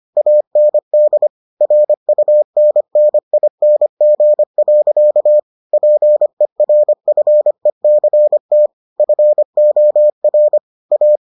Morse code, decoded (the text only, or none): AND RUNNING. PERFECT FOR A